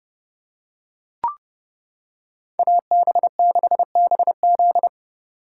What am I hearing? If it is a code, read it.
Morse code, heard as a keyed tone, about 28 wpm, 700 Hz. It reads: AB6BZ